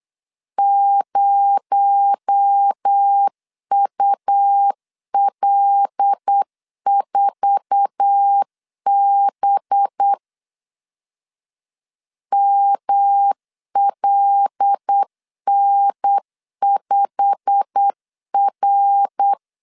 Morse code, heard as a keyed tone, about 8 wpm, 800 Hz. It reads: 0UL4B MLN5R
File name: 8wpm